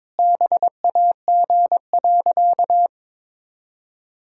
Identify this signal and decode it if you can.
Morse code, heard as a keyed tone, about 22 wpm, 700 Hz. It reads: BAG.
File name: Sentences with 7-Word Phrases - 22wpm